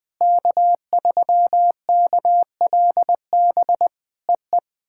Morse code, heard as a keyed tone, about 20 wpm, 700 Hz. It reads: K3KLB EE
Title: POTA QSO Practice - 20wpm